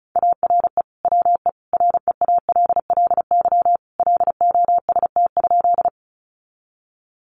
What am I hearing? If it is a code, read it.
Morse code, heard as a keyed tone, about 34 wpm, 700 Hz. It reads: ARE WE REALLY LOST?
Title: Sentences from Top 1400 Words 35wpm